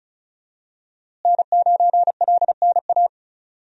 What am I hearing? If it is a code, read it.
Morse code, heard as a keyed tone, about 36 wpm, 700 Hz. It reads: N9LNA
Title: USA Call Signs - 5 Characters - Repeated 2x - 35wpm